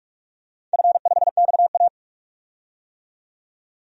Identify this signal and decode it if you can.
Morse code, heard as a keyed tone, about 46 wpm, 700 Hz. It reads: F5XA
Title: DXCC-100 Call Signs - 4 Characters - 45wpm